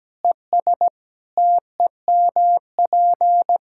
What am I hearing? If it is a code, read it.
Morse code, heard as a keyed tone, about 17 wpm, 700 Hz. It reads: ES TEMP
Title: Practice QSO - Exchange 2 - Info - 17wpm